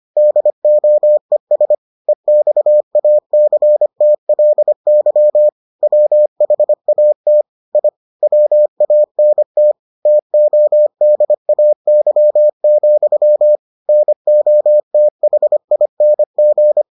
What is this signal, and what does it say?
Morse code, heard as a keyed tone, about 25 wpm, 600 Hz. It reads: DOES EXACTLY WHAT I WANT TODAY, NOTHING